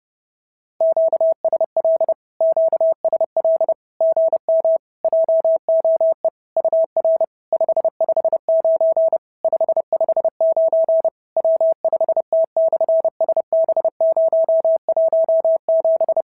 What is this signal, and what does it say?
Morse code, heard as a keyed tone, about 30 wpm, 690 Hz. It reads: QSL QSL GM JOE UR 559 559 W5T/SB017
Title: SOTA QSO Practice 30wpm